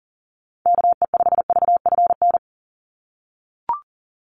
Morse code, heard as a keyed tone, about 38 wpm, 700 Hz. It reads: KE5VFN
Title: USA Call Signs 40wpm